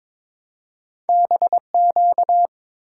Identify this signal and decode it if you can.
Morse code, heard as a keyed tone, about 21 wpm, 700 Hz. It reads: BQ
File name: practice / 22wpm / Two Letter-Number Combinations - 22wpm